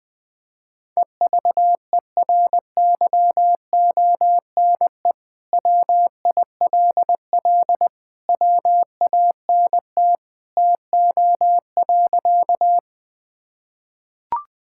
Morse code, heard as a keyed tone, about 21 wpm, 700 Hz. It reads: EVERYONE WILL WANT TO.
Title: Sentences from Top 1000 Words - Review Part 5 - 20wpm